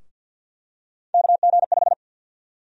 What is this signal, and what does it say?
Morse code, heard as a keyed tone, about 53 wpm, 700 Hz. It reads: KG5
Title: USA Call Sign Prefixes - Repeated 2x - 50wpm